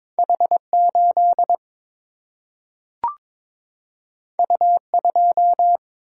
Morse code, heard as a keyed tone, about 22 wpm, 700 Hz. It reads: H 8 U 2
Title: Single-Letter Single-Number 22wpm